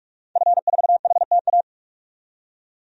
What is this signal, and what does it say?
Morse code, heard as a keyed tone, about 42 wpm, 700 Hz. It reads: F4HTA
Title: DXCC-100 Call Signs - 5 Characters - 45wpm